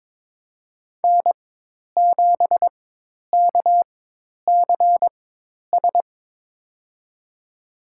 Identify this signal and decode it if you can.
Morse code, heard as a keyed tone, about 22 wpm, spaced as for 10 wpm, 700 Hz. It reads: N7KCS
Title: USA Call Signs - 4x Character Spacing - 22wpm